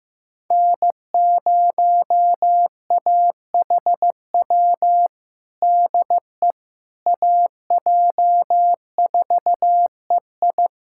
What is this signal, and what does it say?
Morse code, heard as a keyed tone, about 16 wpm, 700 Hz. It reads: N0AHW DE AJ4EI
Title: USA Callsign DE Callsign - 5 Characters - 15wpm